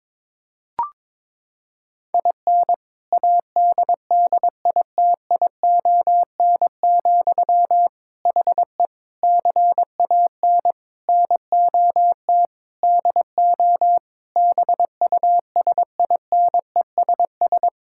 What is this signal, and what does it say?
Morse code, heard as a keyed tone, about 22 wpm, 700 Hz. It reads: IN ADDITION, HE CAN NOT DO BUSINESS
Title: Sentences from Top 1000 Words - Review Part 6 - 22wpm